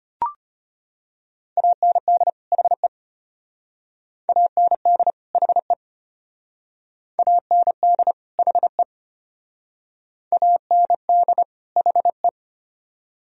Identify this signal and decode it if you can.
Morse code, heard as a keyed tone, about 31 wpm, 700 Hz. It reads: AND HE AND HE AND HE AND HE